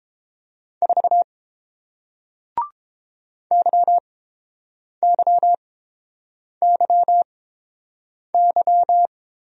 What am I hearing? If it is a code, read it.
Morse code, heard as a keyed tone, about 31 wpm, spaced as for 7 wpm, 700 Hz. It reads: 4 YYYY